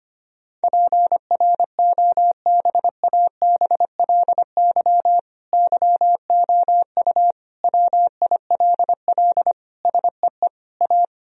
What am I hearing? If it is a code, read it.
Morse code, heard as a keyed tone, about 25 wpm, 700 Hz. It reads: PROBABLY YOU WILL SEE A
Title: Sentences from Top 1000 Words - Review Part 4 - 25wpm